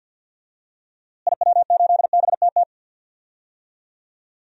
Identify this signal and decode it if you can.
Morse code, heard as a keyed tone, about 52 wpm, 700 Hz. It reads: IW8BTT
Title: DXCC-100 Call Signs - 6 Characters - Repeated 3x - 50wpm